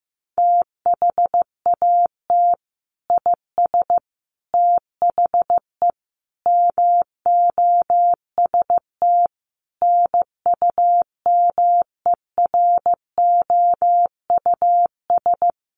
Morse code, heard as a keyed tone, about 15 wpm, 700 Hz. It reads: THAT IS THE MOST NUMEROUS
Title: Sentences from Top 1400 Words 15wpm